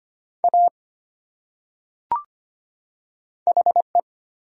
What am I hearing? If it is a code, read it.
Morse code, heard as a keyed tone, about 25 wpm, 700 Hz. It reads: A HE
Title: Top 100 Words 25wpm